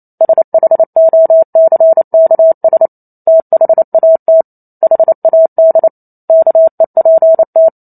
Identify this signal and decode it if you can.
Morse code, heard as a keyed tone, about 28 wpm, 650 Hz. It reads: SHOCKS THAT HAD KEPT